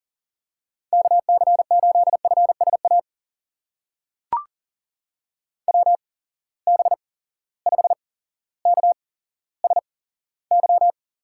Morse code, heard as a keyed tone, about 40 wpm, spaced as for 17 wpm, 700 Hz. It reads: KC8FSA WB5KSY
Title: USA Call Signs - 8x Character Spacing - 40wpm